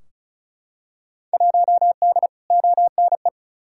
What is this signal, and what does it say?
Morse code, heard as a keyed tone, about 34 wpm, 700 Hz. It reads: 1D ONE
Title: Field Day Exchanges - 35wpm